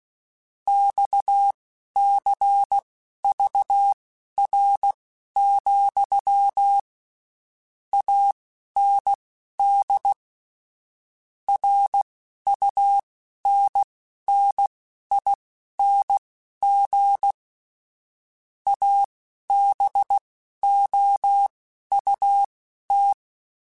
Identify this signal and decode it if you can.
Morse code, heard as a keyed tone, about 16 wpm, spaced as for 11 wpm, 800 Hz. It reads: XCVR, AND RUNNING ABOUT
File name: QSOs / QSOs-mp3s / WD0FBT 1@15